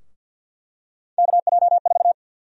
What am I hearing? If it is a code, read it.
Morse code, heard as a keyed tone, about 46 wpm, 700 Hz. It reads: KJ4
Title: USA Call Sign Prefixes - 50wpm